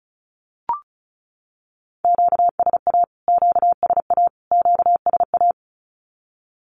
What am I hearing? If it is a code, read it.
Morse code, heard as a keyed tone, about 34 wpm, 700 Hz. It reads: QSA QSA QSA
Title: Q-Codes Repeated 3x 35wpm